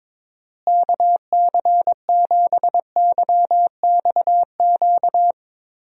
Morse code, heard as a keyed tone, about 22 wpm, 700 Hz. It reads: KC7YXQ
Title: USA Call Signs - 6 Characters - Repeated 3x - 22wpm